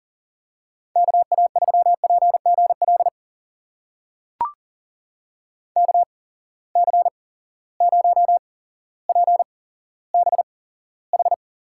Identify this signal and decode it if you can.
Morse code, heard as a keyed tone, about 40 wpm, spaced as for 17 wpm, 700 Hz. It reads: KA3PGL KC0PBH
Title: USA Call Signs - 6 Characters - 8x Character Spacing - 40wpm